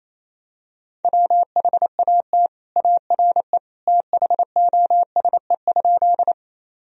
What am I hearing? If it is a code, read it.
Morse code, heard as a keyed tone, about 27 wpm, 700 Hz. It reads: WHAT ARE THOSE?
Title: Sentences from Top 1000 Words - Review Part 1 - 28wpm